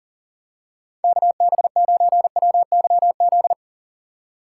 Morse code, heard as a keyed tone, about 37 wpm, 700 Hz. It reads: KB9WYZ
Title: USA Call Signs - 6 Characters - 3x Character Spacing - 40wpm